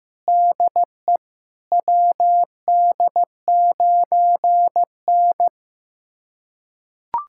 Morse code, heard as a keyed tone, about 15 wpm, 700 Hz. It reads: DE WD9N
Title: USA Callsign DE Callsign - 4 Characters - 15wpm